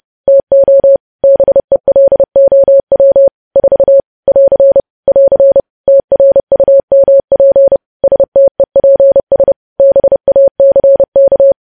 Morse code, heard as a keyed tone, about 30 wpm, 550 Hz. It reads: TO BELOW 4 <AR> <AR> TRUMP STEPS BACK
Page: news headlines in morse code at 30 wpm